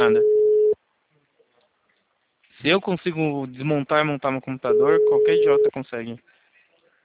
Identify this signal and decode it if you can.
Ringback tone (Europe, 425 Hz), with speech.